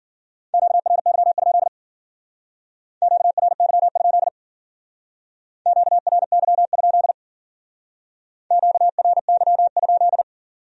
Morse code, heard as a keyed tone, about 49 wpm, 700 Hz. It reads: QRY? QRY? QRY? QRY?
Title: Speed-Racing Q-Codes - 40wpm